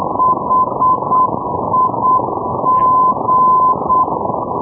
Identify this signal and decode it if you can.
Morse code, heard as a keyed tone, about 8 wpm, 1020 Hz, with noise. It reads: HIG
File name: europe / d-3500 ndb bremen hig-318-hh-2012-03-rb